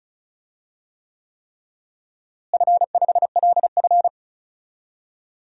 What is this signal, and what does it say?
Morse code, heard as a keyed tone, about 36 wpm, 700 Hz. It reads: F5LF
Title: DXCC-100 Call Signs - 4 Characters - Repeated 3x - 35wpm